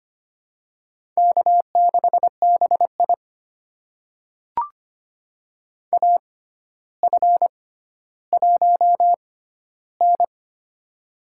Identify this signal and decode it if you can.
Morse code, heard as a keyed tone, about 25 wpm, spaced as for 13 wpm, 700 Hz. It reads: K6BI AF1N
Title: USA Call Signs - 4 Characters - 6x Character Spacing - 25wpm